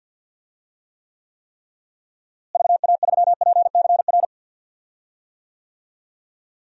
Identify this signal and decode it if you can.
Morse code, heard as a keyed tone, about 52 wpm, 700 Hz. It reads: VA3PCR
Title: DXCC-100 Call Signs - 6 Characters - Repeated 3x - 50wpm